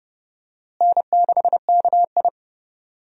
Morse code, heard as a keyed tone, about 30 wpm, 700 Hz. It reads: N6KI
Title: USA Call Signs - 4 Characters - Repeated 3x - 30wpm